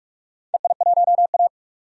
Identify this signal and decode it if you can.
Morse code, heard as a keyed tone, about 42 wpm, 700 Hz. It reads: EI1A
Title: DXCC-100 Call Signs - 4 Characters - 45wpm